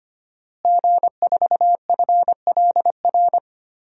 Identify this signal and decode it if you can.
Morse code, heard as a keyed tone, about 25 wpm, 700 Hz. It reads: G4FLR
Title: DXCC-100 Call Signs - 5 Characters - Repeated 3x - 25wpm